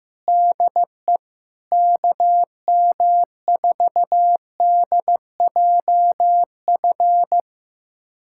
Morse code, heard as a keyed tone, about 15 wpm, 700 Hz. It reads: DE KM4DJF
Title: USA Callsign DE Callsign 15wpm